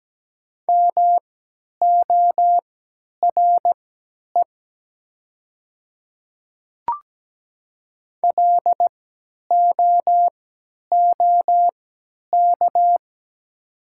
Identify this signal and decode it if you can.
Morse code, heard as a keyed tone, about 17 wpm, spaced as for 10 wpm, 700 Hz. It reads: MORE LOOK
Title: 3x Character Spacing - Top 100 Words - 17wpm